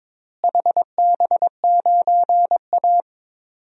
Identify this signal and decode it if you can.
Morse code, heard as a keyed tone, about 22 wpm, 700 Hz. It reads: HB9A
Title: DXCC-100 Call Signs - 4 Characters - Repeated 2x - 22wpm